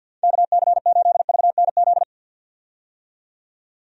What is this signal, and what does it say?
Morse code, heard as a keyed tone, about 52 wpm, 700 Hz. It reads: KC8VNZ
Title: USA Call Signs - 6 Characters - 50wpm